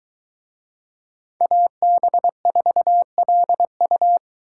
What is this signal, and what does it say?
Morse code, heard as a keyed tone, about 21 wpm, 700 Hz. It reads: AB4LU